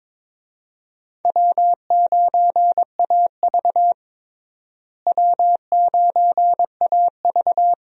Morse code, heard as a keyed tone, about 22 wpm, 700 Hz. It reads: W9AV W9AV
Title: USA Call Signs - 4 Characters - Repeated 2x - 22wpm